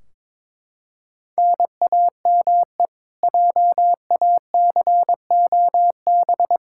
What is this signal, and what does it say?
Morse code, heard as a keyed tone, about 22 wpm, 700 Hz. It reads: NAME JACOB
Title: K1USN SST QSO Practice 22wpm